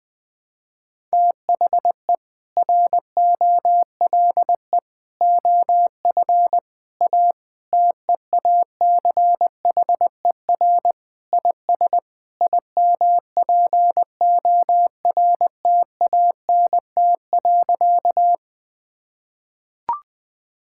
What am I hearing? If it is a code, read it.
Morse code, heard as a keyed tone, about 20 wpm, 700 Hz. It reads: THE ROLE OF A TEACHER IS IMPORTANT.
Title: Sentences with 3-Word Phrases - Spoken Beginning - 20wpm